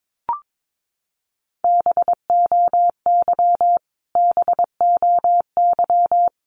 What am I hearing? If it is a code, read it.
Morse code, heard as a keyed tone, about 22 wpm, 700 Hz. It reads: BOY BOY
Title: Top 300 Words Repeated 22wpm